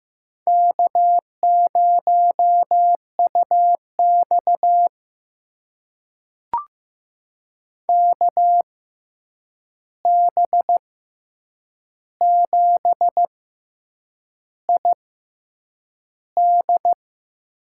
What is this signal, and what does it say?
Morse code, heard as a keyed tone, about 15 wpm, spaced as for 7 wpm, 700 Hz. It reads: K0UX KB7ID
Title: USA Call Signs - 6x Character Spacing - 15wpm